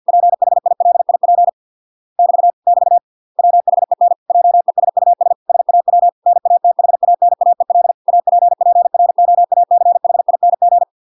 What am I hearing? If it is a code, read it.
Morse code, heard as a keyed tone, about 50 wpm, 700 Hz. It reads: PHILIP <BT> <BT> WHEN JESUS SAW NATHANAEL APPROACHING